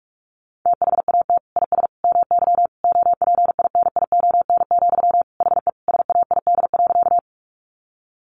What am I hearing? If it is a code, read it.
Morse code, heard as a keyed tone, about 44 wpm, 700 Hz. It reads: THAT IS MY OPINION, HE SAID.
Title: Sentences from Top 1200 Words 45wpm